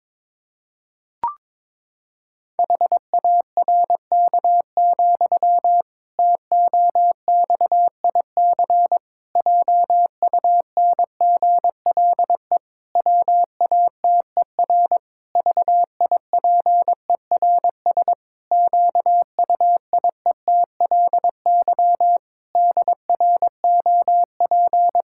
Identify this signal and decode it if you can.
Morse code, heard as a keyed tone, about 22 wpm, 700 Hz. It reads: HARK, TOXIC JUNGLE WATER VIPERS QUIETLY DROP
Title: Pangram Challenge v2 - 22wpm